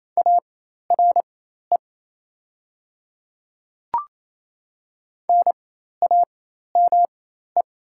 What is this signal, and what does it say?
Morse code, heard as a keyed tone, about 27 wpm, spaced as for 13 wpm, 700 Hz. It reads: ARE NAME
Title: 4x Character Spacing - Top 200 Words - 28wpm